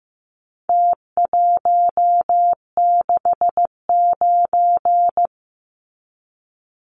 Morse code, heard as a keyed tone, about 15 wpm, 700 Hz. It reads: T169
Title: Numbers - Contest Style - 4-Digit 15wpm